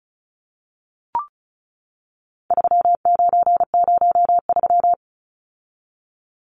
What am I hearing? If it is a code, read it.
Morse code, heard as a keyed tone, about 34 wpm, 700 Hz. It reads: 3903
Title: Numbers - 4-Digit 35wpm